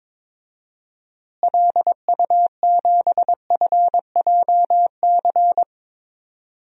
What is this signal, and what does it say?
Morse code, heard as a keyed tone, about 22 wpm, 700 Hz. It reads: LU7FJC
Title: DXCC-100 Call Signs - 6 Characters - Repeated 3x - 22wpm